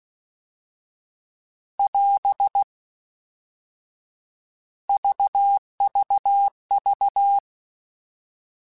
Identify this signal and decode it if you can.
Morse code, heard as a keyed tone, about 16 wpm, 800 Hz. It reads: <AS> VVV
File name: Alphabetical / Alphabetical mp3s / Rs @15